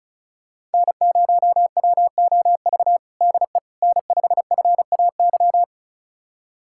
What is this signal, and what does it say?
Morse code, heard as a keyed tone, about 36 wpm, 700 Hz. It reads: N0WOV DE N5FAY
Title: USA Callsign DE Callsign - 5 Characters - 35wpm